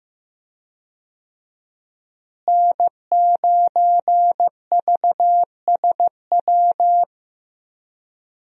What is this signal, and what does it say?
Morse code, heard as a keyed tone, about 15 wpm, 700 Hz. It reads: N9VSW
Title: USA Call Signs - 5 Characters - Repeated 2x - 15wpm